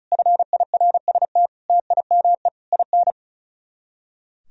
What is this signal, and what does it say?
Morse code, heard as a keyed tone, about 34 wpm, 700 Hz. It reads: FIRST TIME IN
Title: Speed-Racing - Sets of 4 Words 35wpm